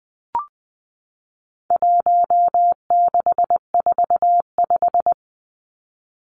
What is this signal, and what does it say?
Morse code, heard as a keyed tone, about 20 wpm, 700 Hz. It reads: 1645
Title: Numbers - 4-Digit 20wpm